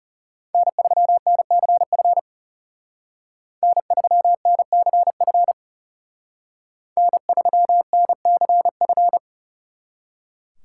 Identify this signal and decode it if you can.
Morse code, heard as a keyed tone, about 36 wpm, 700 Hz. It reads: N3NCF N3NCF N3NCF
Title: Speed Racing - USA 1x3 Call Signs 30wpm